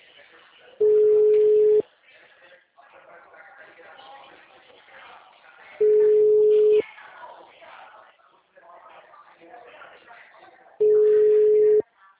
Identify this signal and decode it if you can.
Ringback tone (Europe, 425 Hz), with speech.